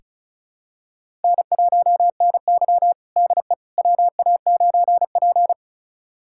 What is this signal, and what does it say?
Morse code, heard as a keyed tone, about 36 wpm, 700 Hz. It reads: N1NY DE WA9P
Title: USA Callsign DE Callsign - 4 Characters - 35wpm